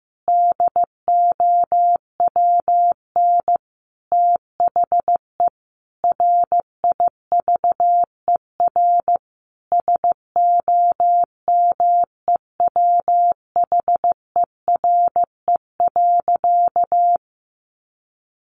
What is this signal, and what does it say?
Morse code, heard as a keyed tone, about 15 wpm, 700 Hz. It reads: DOWN THE RIVER SOMEWHERE.
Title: Sentences from Top 1400 Words 15wpm